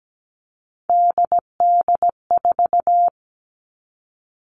Morse code, heard as a keyed tone, about 17 wpm, 700 Hz. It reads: DD4
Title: Three Challenging Letter-Number Combinations 17wpm